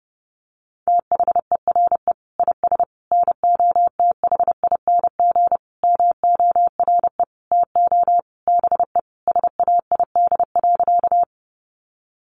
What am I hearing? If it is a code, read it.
Morse code, heard as a keyed tone, about 30 wpm, 700 Hz. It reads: THERE IS NOTHING MORE TO BE SAID.
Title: Sentences from Top 500 Words 30wpm